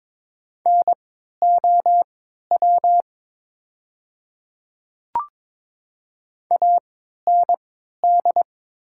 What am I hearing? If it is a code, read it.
Morse code, heard as a keyed tone, about 22 wpm, spaced as for 12 wpm, 700 Hz. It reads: NOW AND
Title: 3x Character Spacing - Top 100 Words - 22wpm